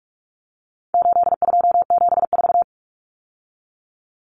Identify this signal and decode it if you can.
Morse code, heard as a keyed tone, about 44 wpm, 700 Hz. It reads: 8274
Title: Numbers - 4-Digit 45wpm